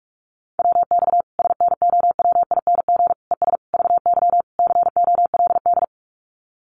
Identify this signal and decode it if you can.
Morse code, heard as a keyed tone, about 44 wpm, 700 Hz. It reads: WX SNOWING ES VY COLD
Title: Common QSO Phrases 45wpm